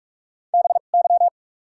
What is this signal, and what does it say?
Morse code, heard as a keyed tone, about 43 wpm, spaced as for 31 wpm, 700 Hz. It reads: BY
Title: 2x Character Spacing - Top 100 Words - 45wpm